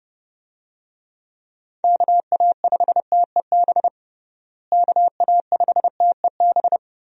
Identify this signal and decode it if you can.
Morse code, heard as a keyed tone, about 31 wpm, 700 Hz. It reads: KA5TEB KA5TEB
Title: USA Call Signs - 6 Characters - Repeated 2x - 30wpm